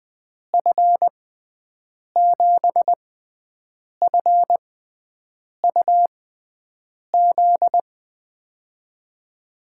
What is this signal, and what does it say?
Morse code, heard as a keyed tone, about 20 wpm, spaced as for 7 wpm, 700 Hz. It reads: F7FUZ
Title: USA Call Signs - 6 Characters - 6x Character Spacing - 20wpm